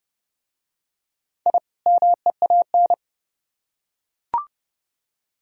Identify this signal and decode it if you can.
Morse code, heard as a keyed tone, about 30 wpm, 700 Hz. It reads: I MEAN